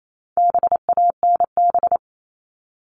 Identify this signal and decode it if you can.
Morse code, heard as a keyed tone, about 28 wpm, 700 Hz. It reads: BANB
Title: All Cut Numbers - 4-Digit 28wpm